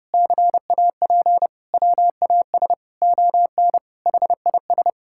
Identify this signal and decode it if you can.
Morse code, heard as a keyed tone, about 32 wpm, 700 Hz. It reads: CAP WAS ON HIS
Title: Sentences from Top 1600 Words 30wpm